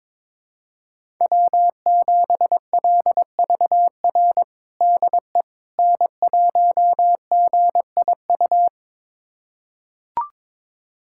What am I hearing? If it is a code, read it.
Morse code, heard as a keyed tone, about 21 wpm, 700 Hz. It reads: W7LVR DE N1GIU
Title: USA Callsign DE Callsign - 5 Characters - 22wpm